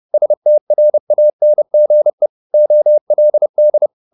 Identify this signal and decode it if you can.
Morse code, heard as a keyed tone, about 30 wpm, 600 Hz. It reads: STRANGE OLD
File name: the old man and the sea / the old man and the sea 1 30wpm